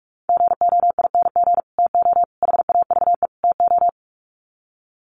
Morse code, heard as a keyed tone, about 44 wpm, 700 Hz. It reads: GOING TO HAVE TO
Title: Sets of 4 Words 45wpm